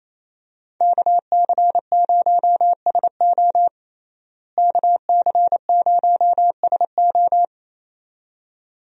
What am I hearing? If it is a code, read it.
Morse code, heard as a keyed tone, about 27 wpm, 700 Hz. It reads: KC0SO KC0SO
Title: USA Call Signs - 5 Characters - Repeated 2x - 28wpm